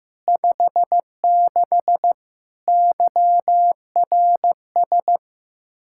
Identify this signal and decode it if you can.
Morse code, heard as a keyed tone, about 16 wpm, 700 Hz. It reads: 56 YRS
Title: Practice QSO - Exchange 3 - More Info - 15wpm